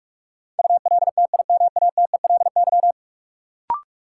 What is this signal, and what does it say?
Morse code, heard as a keyed tone, about 46 wpm, 700 Hz. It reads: ULTIMATELY
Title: Sentences from Top 1200 Words - New Word Drill Alt - 45wpm